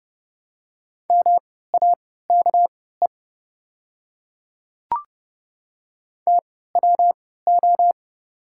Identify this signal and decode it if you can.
Morse code, heard as a keyed tone, about 30 wpm, spaced as for 17 wpm, 700 Hz. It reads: MAKE TWO
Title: 3x Character Spacing - Top 100 Words - 30wpm